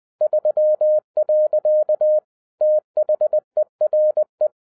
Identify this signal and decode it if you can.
Morse code, heard as a keyed tone, about 20 wpm, 610 Hz. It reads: 3. THERE
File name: 2011-10-mp3 / n13-TIPS FROM MY 34 YEARS OF EXPERIENCE 20wpm